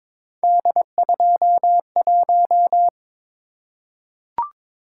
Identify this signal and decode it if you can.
Morse code, heard as a keyed tone, about 22 wpm, 700 Hz. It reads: D21
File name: practice / 22wpm / Three Letter-Number Combinations - 22wpm